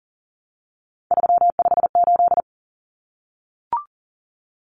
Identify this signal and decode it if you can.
Morse code, heard as a keyed tone, about 40 wpm, 700 Hz. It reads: 358
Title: Numbers - 3-Digit 40wpm